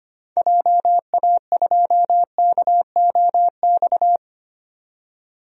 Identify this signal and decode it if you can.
Morse code, heard as a keyed tone, about 25 wpm, 700 Hz. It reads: JA2KOX
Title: DXCC-100 Call Signs - 6 Characters - 25wpm